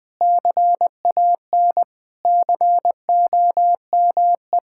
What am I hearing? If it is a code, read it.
Morse code, heard as a keyed tone, about 21 wpm, 700 Hz. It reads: CAN COME
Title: Sentences with 5-Word Phrases - 20wpm